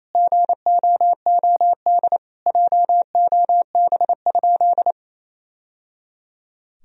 Morse code, heard as a keyed tone, about 28 wpm, 700 Hz. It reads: GOOD JOB?
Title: Sentences from Top 1000 Words - Review Part 3 - 28wpm